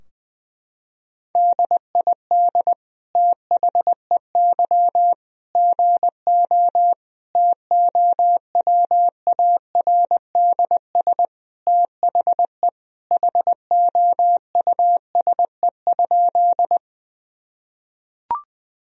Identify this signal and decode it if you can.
Morse code, heard as a keyed tone, about 20 wpm, 700 Hz. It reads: DID THEY GO TOWARDS THE HOUSE?
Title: Sentences from Top 1000 Words - Review Part 2 - 20wpm